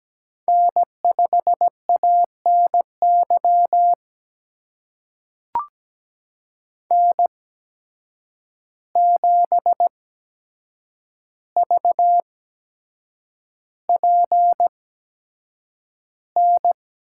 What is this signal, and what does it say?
Morse code, heard as a keyed tone, about 17 wpm, spaced as for 7 wpm, 700 Hz. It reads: N5ANY N7VPN
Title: USA Call Signs - 5 Characters - 8x Character Spacing - 17wpm